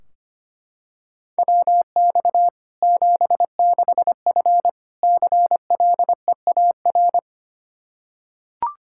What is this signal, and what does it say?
Morse code, heard as a keyed tone, about 25 wpm, 700 Hz. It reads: WX 76F CLEAR
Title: K1USN SST QSO Practice 25wpm